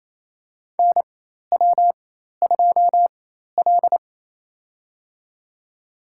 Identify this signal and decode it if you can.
Morse code, heard as a keyed tone, about 28 wpm, spaced as for 13 wpm, 700 Hz. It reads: NW2L
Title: USA Call Signs - 4 Characters - 4x Character Spacing - 28wpm